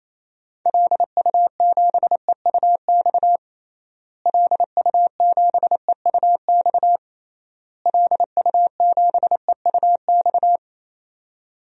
Morse code, heard as a keyed tone, about 28 wpm, 700 Hz. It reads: LU7EUX LU7EUX LU7EUX
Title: DXCC-100 Call Signs - 6 Characters - Repeated 3x - 28wpm